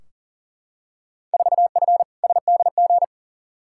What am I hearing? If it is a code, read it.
Morse code, heard as a keyed tone, about 40 wpm, 700 Hz. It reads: 4F SDG
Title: Field Day Exchanges - 40wpm